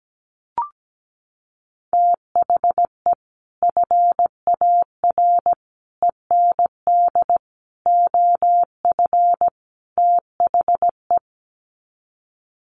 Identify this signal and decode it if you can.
Morse code, heard as a keyed tone, about 17 wpm, 700 Hz. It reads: THE FAR END OF THE
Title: Sets of 5 Words 17wpm